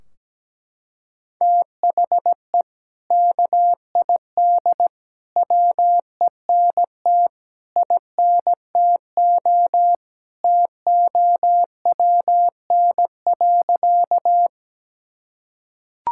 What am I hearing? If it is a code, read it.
Morse code, heard as a keyed tone, about 17 wpm, 700 Hz. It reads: THE KID WENT INTO TOWN.
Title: Sentences from Top 1600 Words 17wpm